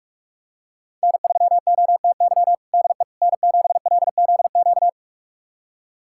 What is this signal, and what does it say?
Morse code, heard as a keyed tone, about 46 wpm, 700 Hz. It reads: N3OTY DE N7LZQ
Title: USA Callsign DE Callsign - 5 Characters - 45wpm